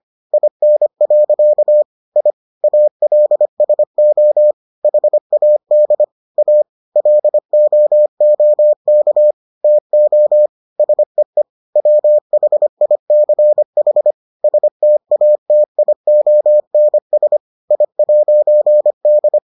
Morse code, heard as a keyed tone, about 25 wpm, 600 Hz. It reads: IN. I ALSO HAD A LOOK TO SEE WHICH STATIONS I'D